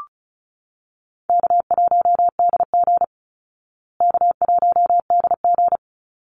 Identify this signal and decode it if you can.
Morse code, heard as a keyed tone, about 34 wpm, 700 Hz. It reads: K1DG K1DG
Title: Call Signs Repeated 2x - 35wpm